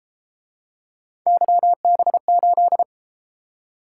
Morse code, heard as a keyed tone, about 29 wpm, 700 Hz. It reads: YB8